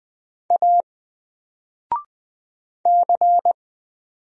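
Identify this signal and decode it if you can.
Morse code, heard as a keyed tone, about 20 wpm, 700 Hz. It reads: A C
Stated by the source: Single letter and number Morse drill at 20 wpm, from N4LCD.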